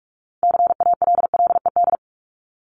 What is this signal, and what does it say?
Morse code, heard as a keyed tone, about 44 wpm, 700 Hz. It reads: CALLED
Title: ED Suffix - Top 100 Words with Base Words 45wpm